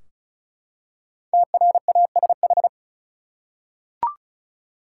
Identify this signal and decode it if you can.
Morse code, heard as a keyed tone, about 36 wpm, 700 Hz. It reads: TRASH
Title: Five Letter Words - 35wpm